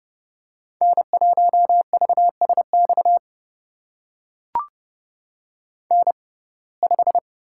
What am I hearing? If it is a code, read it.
Morse code, heard as a keyed tone, about 31 wpm, spaced as for 22 wpm, 700 Hz. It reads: N1VSX N5
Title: USA Call Signs - 5 Characters - 6x Character Spacing - 30wpm